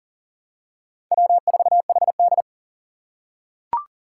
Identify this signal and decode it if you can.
Morse code, heard as a keyed tone, about 40 wpm, 700 Hz. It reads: W4HD
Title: Four Challenging Letter-Number Combinations 40wpm